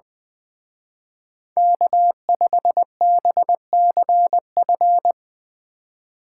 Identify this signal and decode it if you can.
Morse code, heard as a keyed tone, about 23 wpm, 700 Hz. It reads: K5BCF
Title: Speed Racing - USA Call Signs 20wpm